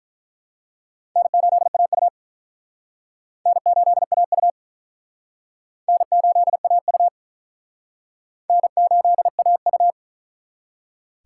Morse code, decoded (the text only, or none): N8AU N8AU N8AU N8AU